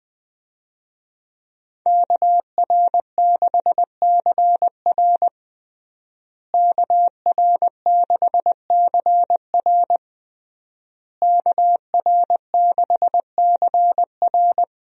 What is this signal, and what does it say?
Morse code, heard as a keyed tone, about 20 wpm, 700 Hz. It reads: KR6CR KR6CR KR6CR
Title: USA Call Signs - 5 Characters - Repeated 3x - 20wpm